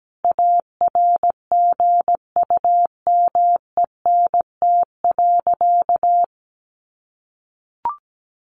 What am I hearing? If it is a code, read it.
Morse code, heard as a keyed tone, about 17 wpm, 700 Hz. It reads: ARGUMENT.
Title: Sentences from Top 1400 Words 17wpm